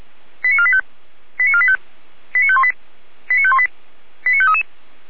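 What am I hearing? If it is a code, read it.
Five-tone selective call (ZVEI), heard as quick runs of five tones, tones of about 70 ms. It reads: 89485 89485 89518 89518 89533
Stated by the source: ZVEI five-tone selective call received on an SDR.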